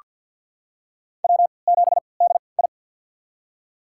Morse code, heard as a keyed tone, about 48 wpm, spaced as for 28 wpm, 690 Hz. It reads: W7DI
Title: USA Call Signs - 3x Character Spacing - 50wpm